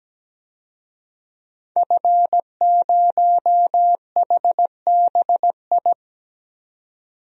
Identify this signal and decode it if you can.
Morse code, heard as a keyed tone, about 17 wpm, 700 Hz. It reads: F0HBI
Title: DXCC-100 Call Signs - 5 Characters - 17wpm